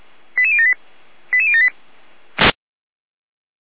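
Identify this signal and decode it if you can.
Five-tone selective call (ZVEI), heard as quick runs of five tones, tones of about 70 ms. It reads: 88078 88078
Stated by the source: ZVEI five-tone selective call received on SDR.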